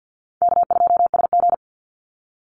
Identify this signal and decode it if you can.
Morse code, heard as a keyed tone, about 50 wpm, 700 Hz. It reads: K2SG
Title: Easy Call Signs Repeated 2x - 50wpm